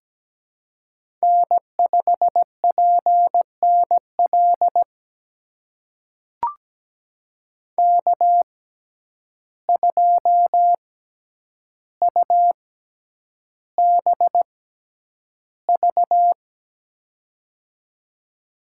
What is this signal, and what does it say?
Morse code, heard as a keyed tone, about 17 wpm, spaced as for 9 wpm, 700 Hz. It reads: N5PNL K2UBV
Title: USA Call Signs - 5 Characters - 6x Character Spacing - 17wpm